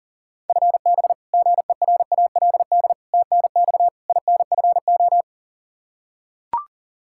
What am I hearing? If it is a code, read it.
Morse code, heard as a keyed tone, about 37 wpm, 700 Hz. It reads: FB GERALD TNX INFO
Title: Practice QSO - Exchange 3 - More Info - 40wpm